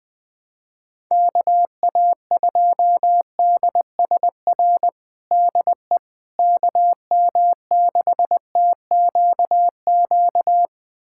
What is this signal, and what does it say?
Morse code, heard as a keyed tone, about 20 wpm, 700 Hz. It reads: KA2DSR DE KM6TQQ
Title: USA Callsign DE Callsign 20wpm